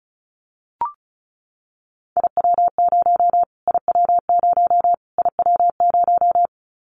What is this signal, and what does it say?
Morse code, heard as a keyed tone, about 34 wpm, 700 Hz. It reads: IW0 IW0 IW0
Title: Callsign Prefixes for Top 25 DXCC - Repeated 3x - 35wpm